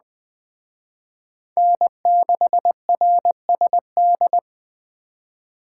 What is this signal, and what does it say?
Morse code, heard as a keyed tone, about 20 wpm, 700 Hz. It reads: N6RSD
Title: Speed Racing - USA 1x3 Call Signs 17wpm